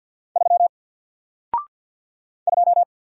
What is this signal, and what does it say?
Morse code, heard as a keyed tone, about 48 wpm, 700 Hz. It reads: 3 2
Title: Single Letter-Number Alt 50wpm